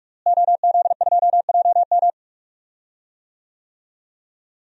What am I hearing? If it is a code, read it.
Morse code, heard as a keyed tone, about 46 wpm, 700 Hz. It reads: OZ2JM
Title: DXCC-100 Call Signs - 5 Characters - Repeated 3x - 45wpm